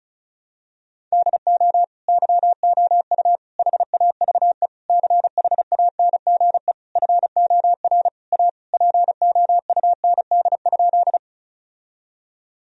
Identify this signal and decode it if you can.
Morse code, heard as a keyed tone, about 36 wpm, 700 Hz. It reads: DO YOU HAVE CHANGE FOR A POUND?
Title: Sentences from Top 1000 Words - Review Part 2 - 35wpm